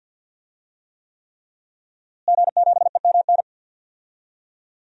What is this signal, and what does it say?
Morse code, heard as a keyed tone, about 52 wpm, 700 Hz. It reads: G7EMN
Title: DXCC-100 Call Signs - 5 Characters - Repeated 3x - 50wpm